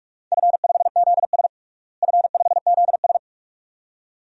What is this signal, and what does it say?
Morse code, heard as a keyed tone, about 46 wpm, 700 Hz. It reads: F5ZS F5ZS
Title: DXCC-100 Call Signs - 4 Characters - Repeated 2x - 45wpm